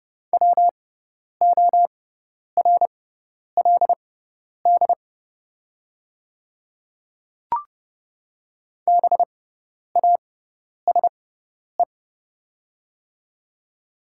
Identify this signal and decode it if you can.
Morse code, heard as a keyed tone, about 29 wpm, spaced as for 10 wpm, 700 Hz. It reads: WORLD BASE
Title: 6x Character Spacing - Top 300 Words - 30wpm